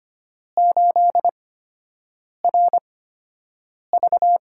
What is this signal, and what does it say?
Morse code, heard as a keyed tone, about 25 wpm, spaced as for 7 wpm, 700 Hz. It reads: 8RV
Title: USA Call Signs - 6 Characters - 8x Character Spacing - 25wpm